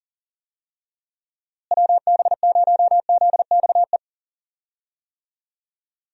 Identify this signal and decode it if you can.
Morse code, heard as a keyed tone, about 40 wpm, 700 Hz. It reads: WB0ZXE
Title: USA Call Signs - 6 Characters - Repeated 3x - 40wpm